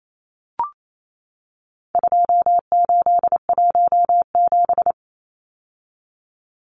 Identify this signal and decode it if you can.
Morse code, heard as a keyed tone, about 28 wpm, 700 Hz. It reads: 2817
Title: Numbers - 4-Digit 28wpm